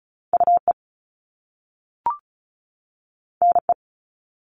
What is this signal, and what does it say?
Morse code, heard as a keyed tone, about 36 wpm, 700 Hz. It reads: UE NE
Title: Top Two Letter Combinations in Words 35wpm